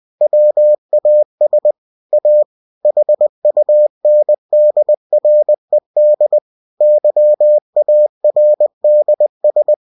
Morse code, heard as a keyed tone, about 20 wpm, 600 Hz. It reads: WAS A HUNDRED YARDS